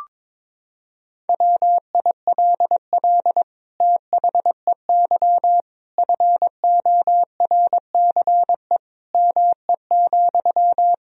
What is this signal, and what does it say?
Morse code, heard as a keyed tone, about 22 wpm, 700 Hz. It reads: WILL THEY FORCE ME,
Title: Sentences from Top 1000 Words - Review Part 3 - 22wpm